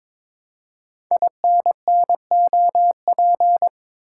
Morse code, heard as a keyed tone, about 22 wpm, 700 Hz. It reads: INNOP
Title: Code Groups - 5 Letters - 22wpm